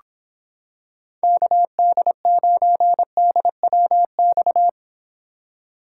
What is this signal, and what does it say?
Morse code, heard as a keyed tone, about 26 wpm, 700 Hz. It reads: KD9DWX